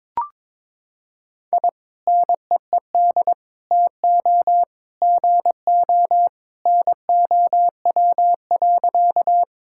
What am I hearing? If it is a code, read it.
Morse code, heard as a keyed tone, about 22 wpm, 700 Hz. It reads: I NEED TO GO NOW.
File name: practice / 22wpm / Sentences with 2-Word Phrases - 22wpm